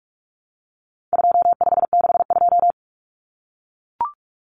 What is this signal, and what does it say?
Morse code, heard as a keyed tone, about 44 wpm, 700 Hz. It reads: 2562
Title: Common Cut Numbers - 4-Digit 45wpm